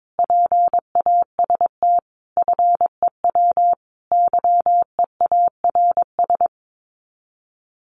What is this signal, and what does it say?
Morse code, heard as a keyed tone, about 22 wpm, 700 Hz. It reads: PAST FEW YEARS
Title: Sets of 5 Words Encore 22wpm